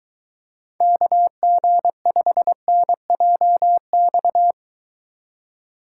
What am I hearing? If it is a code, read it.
Morse code, heard as a keyed tone, about 23 wpm, 700 Hz. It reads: KG5NJX